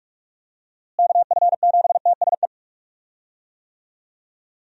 Morse code, heard as a keyed tone, about 45 wpm, 700 Hz. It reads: KF7TSE
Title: USA Call Signs - 6 Characters - Repeated 3x - 45wpm